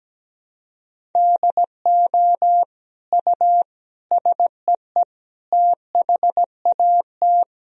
Morse code, heard as a keyed tone, about 17 wpm, 700 Hz. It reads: DO U SEE THAT
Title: Sentences from Top 500 Words and Abbreviations 17wpm